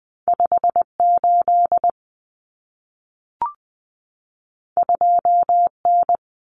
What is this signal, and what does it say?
Morse code, heard as a keyed tone, about 20 wpm, 700 Hz. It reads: 58 2N
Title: Common Cut Numbers - 2-Digit 20wpm